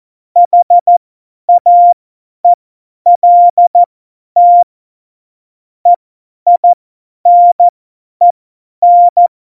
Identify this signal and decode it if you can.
Morse code, heard as a keyed tone, about 14 wpm, spaced as for 10 wpm, 700 Hz. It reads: HAELT EINEN
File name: Morsekurs / Witze / Witz 05. Februar 2019, 50 F70